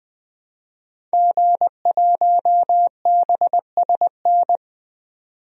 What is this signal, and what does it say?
Morse code, heard as a keyed tone, about 20 wpm, 700 Hz. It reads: G1BSN
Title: DXCC-100 Call Signs - 5 Characters - 20wpm